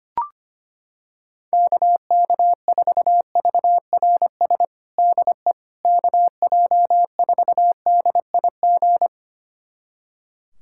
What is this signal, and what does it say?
Morse code, heard as a keyed tone, about 25 wpm, 700 Hz. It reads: KK4VRS DE KJ4DIG
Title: USA Callsign DE Callsign - 6 Characters - 25wpm